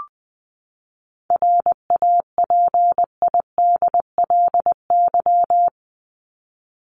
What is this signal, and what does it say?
Morse code, heard as a keyed tone, about 20 wpm, 700 Hz. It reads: RAPIDLY
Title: LY Suffix - Top 100 Words with Base Words 20wpm